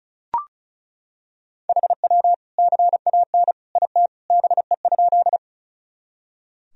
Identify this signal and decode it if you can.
Morse code, heard as a keyed tone, about 34 wpm, 700 Hz. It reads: HW CAN IT BE?
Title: Sentences from Top 100 Words and Abbreviations 35wpm